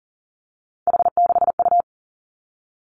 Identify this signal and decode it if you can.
Morse code, heard as a keyed tone, about 41 wpm, 700 Hz. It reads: H6U
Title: Three Challenging Letter-Number Combinations 40wpm